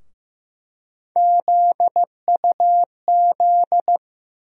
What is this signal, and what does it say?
Morse code, heard as a keyed tone, about 15 wpm, 700 Hz. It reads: ZUZ
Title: USA Call Sign Postfixes - 3 Letters 15wpm